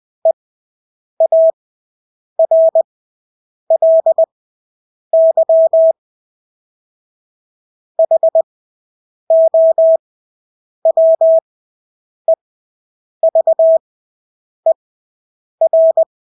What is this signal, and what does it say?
Morse code, heard as a keyed tone, about 20 wpm, spaced as for 8 wpm, 650 Hz. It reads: EARLY HOWEVER
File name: Engelske ord 20/8 wpm